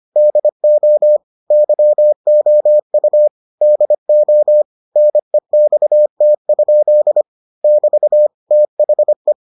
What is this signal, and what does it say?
Morse code, heard as a keyed tone, about 25 wpm, 600 Hz. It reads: DO YOU DO NEXT? <BT> THE